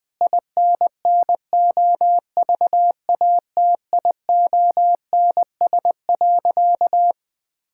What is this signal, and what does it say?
Morse code, heard as a keyed tone, about 20 wpm, 700 Hz. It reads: INNOVATIONS.
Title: Sentences with 8-Word Phrases - 20wpm